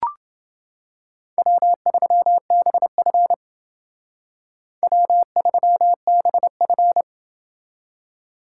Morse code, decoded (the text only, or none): W3BF W3BF